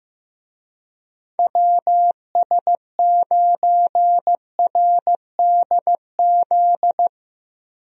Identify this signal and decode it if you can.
Morse code, heard as a keyed tone, about 15 wpm, 700 Hz. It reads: WS9RDZ
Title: USA Call Signs - 6 Characters - Repeated 2x - 15wpm